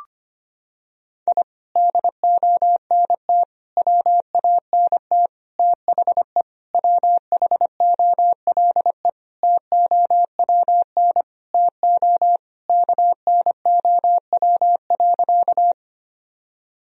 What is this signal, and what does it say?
Morse code, heard as a keyed tone, about 25 wpm, 700 Hz. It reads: I DONT WANT THE WHOLE TOWN TO KNOW.
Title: Sentences from Top 1000 Words - Review Part 2 - 25wpm